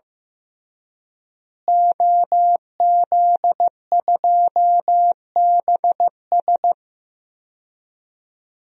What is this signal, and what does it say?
Morse code, heard as a keyed tone, about 15 wpm, 700 Hz. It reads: OZ2BS
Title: DXCC-100 Call Signs - 5 Characters - Repeated 2x - 15wpm